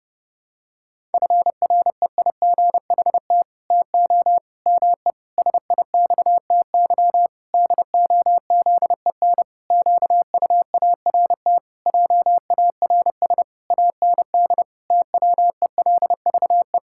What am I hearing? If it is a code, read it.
Morse code, heard as a keyed tone, about 30 wpm, 700 Hz. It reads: FREIGHT TO ME SIXTY DOZEN QUART JARS AND TWELVE
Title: Pangram Challenge v2 - 30wpm